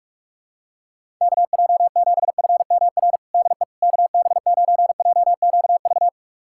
Morse code, heard as a keyed tone, about 41 wpm, 700 Hz. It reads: KJ7FMR DE KB9JQV
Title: USA Callsign DE Callsign 45wpm